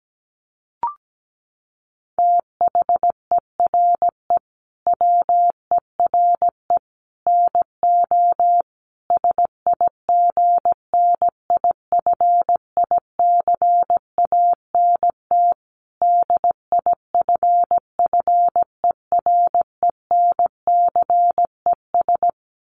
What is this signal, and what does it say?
Morse code, heard as a keyed tone, about 17 wpm, 700 Hz. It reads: THERE WERE NO SIGNIFICANT DIFFERENCES
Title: Sets of 5 Words Encore2 17wpm